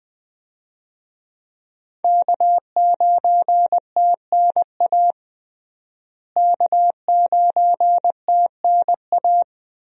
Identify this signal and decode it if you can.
Morse code, heard as a keyed tone, about 20 wpm, 700 Hz. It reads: K9TNA K9TNA
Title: USA Call Signs - 5 Characters - Repeated 3x - 20wpm